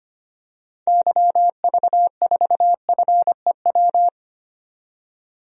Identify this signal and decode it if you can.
Morse code, heard as a keyed tone, about 25 wpm, 700 Hz. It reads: YV4FEW
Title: DXCC-100 Call Signs - 6 Characters - 25wpm